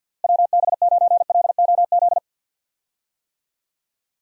Morse code, heard as a keyed tone, about 48 wpm, 700 Hz. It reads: WB9LOZ
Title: USA Call Signs - 6 Characters - Repeated 2x - 50wpm